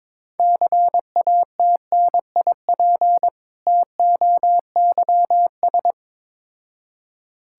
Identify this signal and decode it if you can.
Morse code, heard as a keyed tone, about 22 wpm, 700 Hz. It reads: CATNIP TOYS
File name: Difficult Words with Hints - 22wpm